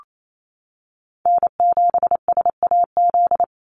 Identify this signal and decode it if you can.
Morse code, heard as a keyed tone, about 28 wpm, 700 Hz. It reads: N7SAZ
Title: USA Call Signs 28wpm